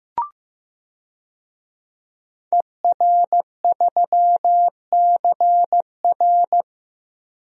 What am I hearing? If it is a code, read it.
Morse code, heard as a keyed tone, about 15 wpm, 700 Hz. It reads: ER3CR
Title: DXCC-100 Call Signs - 5 Characters - Repeated 2x - 15wpm